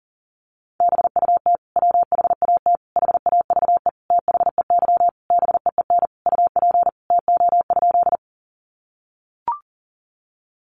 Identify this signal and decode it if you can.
Morse code, heard as a keyed tone, about 40 wpm, 700 Hz. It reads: BUT WHAT HAVE THEY BEEN UP TO?
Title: Sentences from Top 100 Words 40wpm